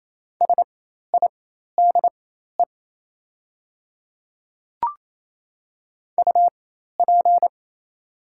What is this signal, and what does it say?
Morse code, heard as a keyed tone, about 28 wpm, spaced as for 13 wpm, 690 Hz. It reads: SIDE UP
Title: 4x Character Spacing - Top 300 Words - 28wpm